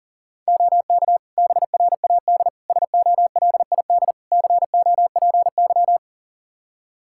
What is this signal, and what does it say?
Morse code, heard as a keyed tone, about 37 wpm, 700 Hz. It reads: OK BRAD SOLID COPY
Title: K1USN SST QSO Practice 40wpm